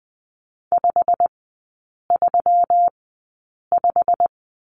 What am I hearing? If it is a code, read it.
Morse code, heard as a keyed tone, about 20 wpm, spaced as for 8 wpm, 690 Hz. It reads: 535
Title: 172 Lesson 28e - Number 9 - Three Characters 20wpm